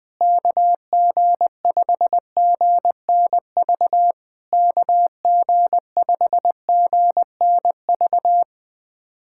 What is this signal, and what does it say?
Morse code, heard as a keyed tone, about 21 wpm, 700 Hz. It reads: KG5GNV KG5GNV
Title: SOTA QSO Practice 20wpm